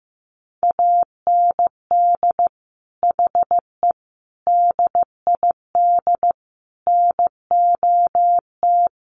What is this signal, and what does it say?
Morse code, heard as a keyed tone, about 15 wpm, 700 Hz. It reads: AND HE DID NOT
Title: Sentences from Top 1500 Words 15wpm